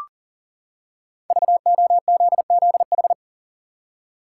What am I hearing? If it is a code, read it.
Morse code, heard as a keyed tone, about 37 wpm, 700 Hz. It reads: VOZZH
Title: Code Groups - 5 Letter:Numbers - 40wpm